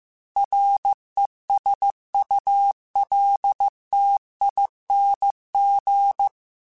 Morse code, heard as a keyed tone, about 15 wpm, 800 Hz. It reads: RESULTING
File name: Technical Subjects / Technical Subjects-mp3s / XL @13